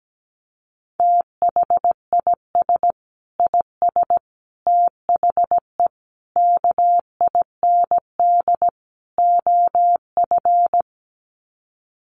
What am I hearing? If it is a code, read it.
Morse code, heard as a keyed tone, about 17 wpm, 700 Hz. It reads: THIS IS THE KIND OF
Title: Sets of 5 Words Encore 17wpm